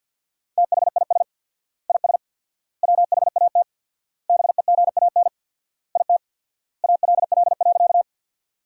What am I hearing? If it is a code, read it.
Morse code, heard as a keyed tone, about 50 wpm, 700 Hz. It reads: THIS IS WHAT BEGAN IT ALL.
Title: 4x Word-Spacing - Sentences from Top 300 Words - 50wpm